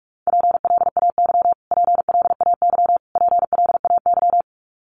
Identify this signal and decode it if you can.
Morse code, heard as a keyed tone, about 47 wpm, 700 Hz. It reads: PLAY PLAY PLAY
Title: Top 500 Words Repeated 45wpm